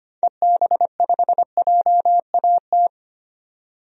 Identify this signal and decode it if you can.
Morse code, heard as a keyed tone, about 25 wpm, 700 Hz. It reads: EB5JAT
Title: DXCC-100 Call Signs - 6 Characters - 25wpm